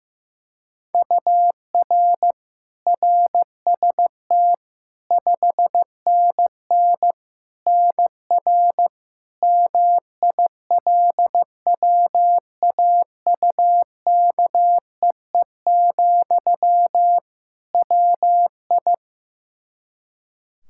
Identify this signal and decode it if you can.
Morse code, heard as a keyed tone, about 15 wpm, 700 Hz. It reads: UR RST 5NN NR MILWAUKEE, WI
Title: Practice QSO - Exchange 1 - Beginning - 15wpm